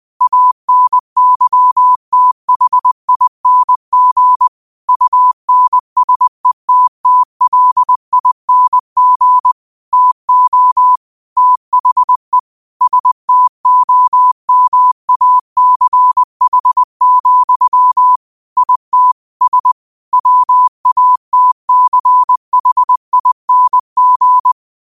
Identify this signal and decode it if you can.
Morse code, heard as a keyed tone, about 20 wpm, 1000 Hz. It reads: ANYTHING UNSETTLING TO THE STOMACH, IT S WATCHING
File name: qotdfiles / 20wpm